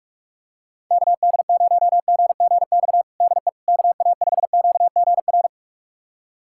Morse code, heard as a keyed tone, about 46 wpm, 700 Hz. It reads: KD0GGX DE KA5QGR
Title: USA Callsign DE Callsign 45wpm